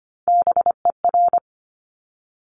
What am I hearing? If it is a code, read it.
Morse code, heard as a keyed tone, about 25 wpm, 700 Hz. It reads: BER
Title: Top Three Letter Combinations in Words 25wpm